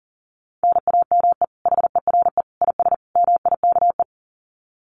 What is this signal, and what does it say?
Morse code, heard as a keyed tone, about 40 wpm, 700 Hz. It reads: NAME HERE IS MIKE
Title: Common QSO Phrases 40wpm